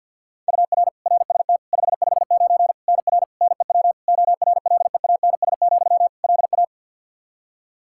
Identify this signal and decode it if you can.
Morse code, heard as a keyed tone, about 52 wpm, 700 Hz. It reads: UR RST 559 NR NEW ORLEANS, LA
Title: K1USN SST QSO Practice 50wpm